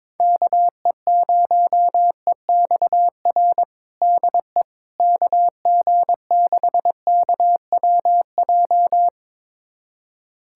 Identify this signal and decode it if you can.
Morse code, heard as a keyed tone, about 22 wpm, 700 Hz. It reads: KE0EXR DE KG6KWJ
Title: USA Callsign DE Callsign - 6 Characters - 22wpm